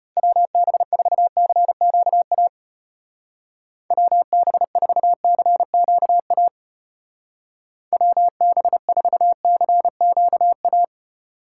Morse code, heard as a keyed tone, about 33 wpm, 700 Hz. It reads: WB4CQA WB4CQA WB4CQA